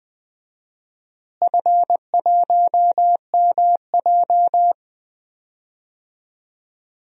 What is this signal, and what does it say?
Morse code, heard as a keyed tone, about 20 wpm, 700 Hz. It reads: F1MJ
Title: DXCC-100 Call Signs - 4 Characters - Repeated 2x - 20wpm